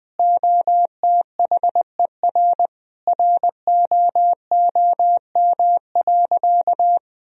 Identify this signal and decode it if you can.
Morse code, heard as a keyed tone, about 21 wpm, 700 Hz. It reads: OTHER ROOM.
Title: Sentences from Top 1000 Words - Review Part 1 - 20wpm